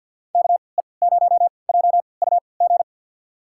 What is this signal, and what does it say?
Morse code, heard as a keyed tone, about 49 wpm, spaced as for 28 wpm, 700 Hz. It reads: KE0JUG
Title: USA Call Signs - 6 Characters - 3x Character Spacing - 50wpm